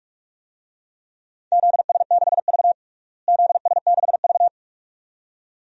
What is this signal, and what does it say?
Morse code, heard as a keyed tone, about 46 wpm, 700 Hz. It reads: ZS6V ZS6V
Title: DXCC-100 Call Signs - 4 Characters - Repeated 2x - 45wpm